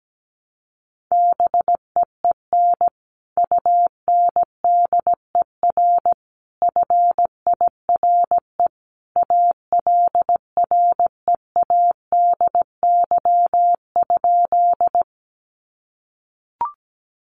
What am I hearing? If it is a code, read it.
Morse code, heard as a keyed tone, about 17 wpm, 700 Hz. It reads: BEEN UNDER FIRE ALREADY?
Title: Sentences from Top 700 Words 17wpm